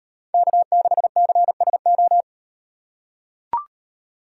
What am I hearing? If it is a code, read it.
Morse code, heard as a keyed tone, about 37 wpm, 700 Hz. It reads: K6CSO